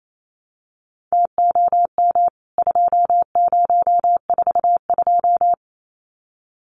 Morse code, heard as a keyed tone, about 28 wpm, 700 Hz. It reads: TOM 2042
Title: CWT Exchange 28wpm